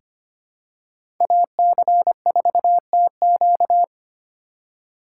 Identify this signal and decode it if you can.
Morse code, heard as a keyed tone, about 25 wpm, 700 Hz. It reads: AC4TQ
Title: USA Call Signs - 2x Character Spacing - 25wpm